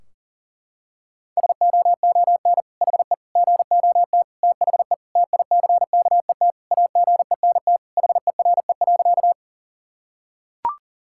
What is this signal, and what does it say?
Morse code, heard as a keyed tone, about 40 wpm, 700 Hz. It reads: SOON HE GOT THE TICKET AGENT HERE.
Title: Sentences from Top 1600 Words 40wpm